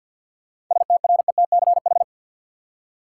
Morse code, heard as a keyed tone, about 48 wpm, 700 Hz. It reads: STRETCH
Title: Seven Letter Words - 50wpm